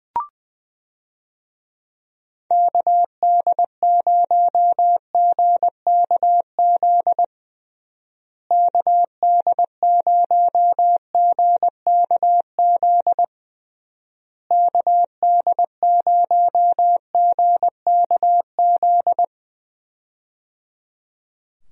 Morse code, heard as a keyed tone, about 21 wpm, 700 Hz. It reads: KD0GKZ KD0GKZ KD0GKZ
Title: USA Call Signs - 6 Characters - Repeated 3x - 20wpm